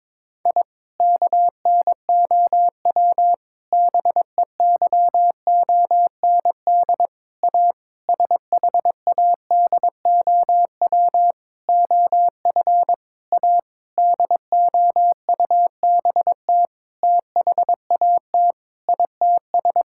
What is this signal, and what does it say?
Morse code, heard as a keyed tone, about 22 wpm, 700 Hz. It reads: I KNOW BEYOND A SHADOW OF A DOUBT THAT ITS
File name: Sentences with 7-Word Phrases - 22wpm